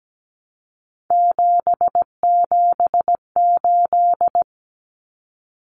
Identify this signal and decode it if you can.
Morse code, heard as a keyed tone, about 17 wpm, 700 Hz. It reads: 778
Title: Common Cut Numbers - 3-Digit 17wpm